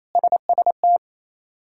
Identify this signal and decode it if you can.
Morse code, heard as a keyed tone, about 28 wpm, 700 Hz. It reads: SST
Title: K1USN SST QSO Practice - 28wpm